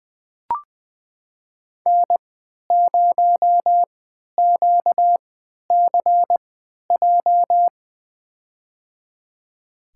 Morse code, heard as a keyed tone, about 20 wpm, spaced as for 11 wpm, 700 Hz. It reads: N0QCJ
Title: USA Call Signs - 5 Characters - 3x Character Spacing - 20wpm